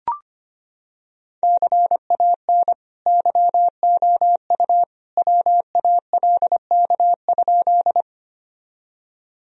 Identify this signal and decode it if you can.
Morse code, heard as a keyed tone, about 25 wpm, 700 Hz. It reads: CAN YOU WALK?
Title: Sentences from Top 1000 Words - Review Part 1 - 25wpm